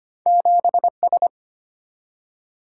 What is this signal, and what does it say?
Morse code, heard as a keyed tone, about 25 wpm, 700 Hz. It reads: 7S
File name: Two Letter-Number Combinations - 25wpm